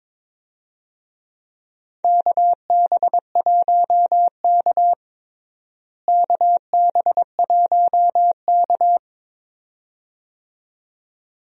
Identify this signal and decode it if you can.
Morse code, heard as a keyed tone, about 21 wpm, 700 Hz. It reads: KB1K KB1K
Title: USA Call Signs - 4 Characters - Repeated 2x - 22wpm